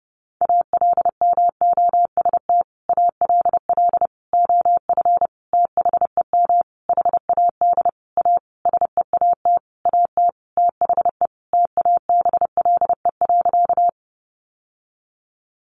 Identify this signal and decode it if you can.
Morse code, heard as a keyed tone, about 30 wpm, 700 Hz. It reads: ALMOST ALL OF THEM HAD A SEAT AT THE TABLE.
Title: Sentences from Top 1500 Words 30wpm